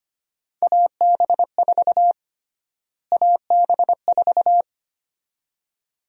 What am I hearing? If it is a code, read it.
Morse code, heard as a keyed tone, about 25 wpm, 700 Hz. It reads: AB4 AB4
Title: USA Call Sign Prefixes - Repeated 2x - 25wpm